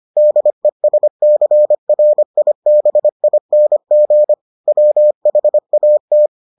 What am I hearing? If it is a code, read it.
Morse code, heard as a keyed tone, about 25 wpm, 600 Hz. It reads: DESCRIBING WHAT